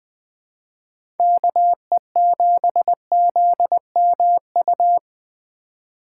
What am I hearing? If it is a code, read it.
Morse code, heard as a keyed tone, about 20 wpm, 700 Hz. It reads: KE7ZMU
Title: USA Call Signs - 6 Characters - 4x Character Spacing - 20wpm